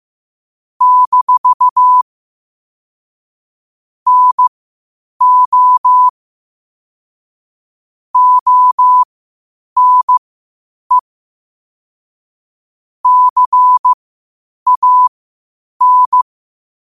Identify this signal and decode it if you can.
Morse code, heard as a keyed tone, about 15 wpm, spaced as for 8 wpm, 1000 Hz. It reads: - NO ONE CAN
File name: qotdfiles / 8wpm